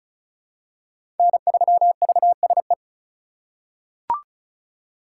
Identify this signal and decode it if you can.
Morse code, heard as a keyed tone, about 34 wpm, 700 Hz. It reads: N3VSE
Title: USA Call Signs - 5 Characters - 3x Character Spacing - 35wpm